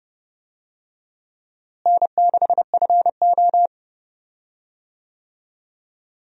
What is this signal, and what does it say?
Morse code, heard as a keyed tone, about 30 wpm, 700 Hz. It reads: N6FO
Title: USA Call Signs - 4 Characters - Repeated 2x - 30wpm